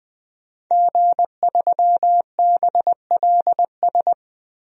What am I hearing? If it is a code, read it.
Morse code, heard as a keyed tone, about 20 wpm, 700 Hz. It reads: G3BLS
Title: DXCC-100 Call Signs - 5 Characters - 20wpm